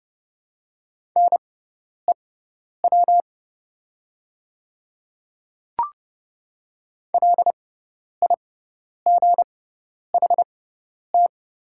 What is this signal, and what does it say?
Morse code, heard as a keyed tone, about 30 wpm, spaced as for 10 wpm, 700 Hz. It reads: NEW LIGHT
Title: 6x Character Spacing - Top 200 Words - 30wpm